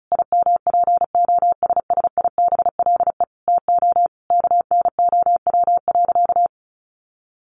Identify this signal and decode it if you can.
Morse code, heard as a keyed tone, about 34 wpm, 700 Hz. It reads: IMPOSSIBLE TO KNOW.
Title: Sentences from Top 1100 Words 35wpm